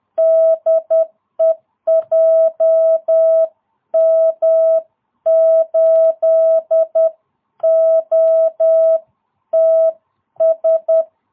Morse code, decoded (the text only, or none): DE JM8OTS